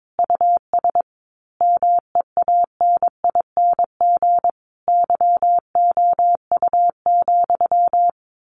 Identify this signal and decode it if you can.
Morse code, heard as a keyed tone, about 22 wpm, 700 Hz. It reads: US MEANING YOU,
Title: A Sound Of Thunder 22wpm